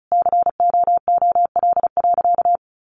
Morse code, heard as a keyed tone, about 34 wpm, 700 Hz. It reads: COOL.
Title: Sentences from Top 1200 Words 35wpm